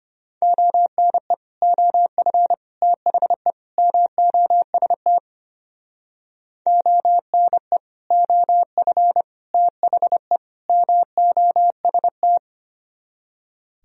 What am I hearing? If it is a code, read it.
Morse code, heard as a keyed tone, about 31 wpm, 700 Hz. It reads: ONE OF THE MOST ONE OF THE MOST